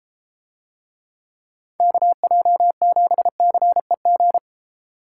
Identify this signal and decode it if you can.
Morse code, heard as a keyed tone, about 32 wpm, 700 Hz. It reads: KJ7CEG